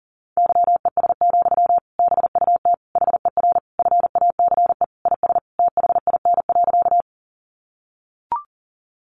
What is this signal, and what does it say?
Morse code, heard as a keyed tone, about 37 wpm, 700 Hz. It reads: YES, BUT HER FACE IS THIN.
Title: Sentences from Top 1200 Words 40wpm